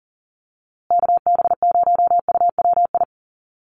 Morse code, heard as a keyed tone, about 40 wpm, 700 Hz. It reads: KB0UWI
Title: USA Call Signs Repeated 2x 40wpm